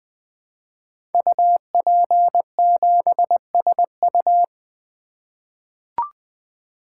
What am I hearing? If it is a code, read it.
Morse code, heard as a keyed tone, about 20 wpm, 700 Hz. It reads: UP7SU
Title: Code Groups - 5 Letter:Numbers - 20wpm